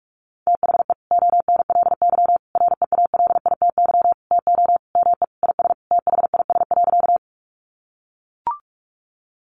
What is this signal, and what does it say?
Morse code, heard as a keyed tone, about 44 wpm, 700 Hz. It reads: THE ONLY REALITY TO ME IS THIS.
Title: Sentences from Top 1200 Words 45wpm